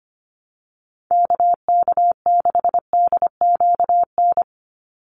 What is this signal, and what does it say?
Morse code, heard as a keyed tone, about 25 wpm, 700 Hz. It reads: KK6DQN
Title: USA Call Signs 25wpm